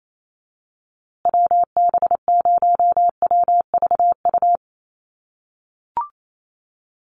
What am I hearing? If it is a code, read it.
Morse code, heard as a keyed tone, about 28 wpm, 700 Hz. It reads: WB0WVU
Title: USA Call Signs Repeated 2x 28wpm